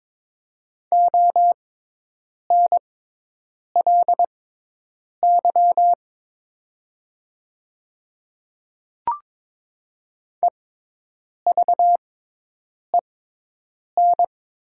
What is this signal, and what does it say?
Morse code, heard as a keyed tone, about 22 wpm, spaced as for 8 wpm, 700 Hz. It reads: ONLY EVEN
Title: 6x Character Spacing - Top 300 Words - 22wpm